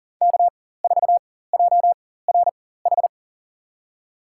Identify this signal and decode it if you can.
Morse code, heard as a keyed tone, about 40 wpm, spaced as for 19 wpm, 700 Hz. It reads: K4JRH
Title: USA Call Signs - 4x Character Spacing - 40wpm